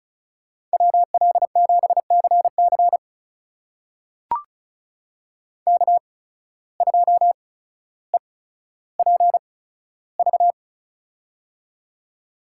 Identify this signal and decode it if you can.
Morse code, heard as a keyed tone, about 36 wpm, spaced as for 15 wpm, 700 Hz. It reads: WL7CC K2EPV
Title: USA Call Signs - 8x Character Spacing - 35wpm